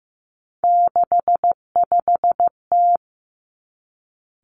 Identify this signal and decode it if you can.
Morse code, heard as a keyed tone, about 15 wpm, 700 Hz. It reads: 65T
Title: Three Challenging Letter-Number Combinations 15wpm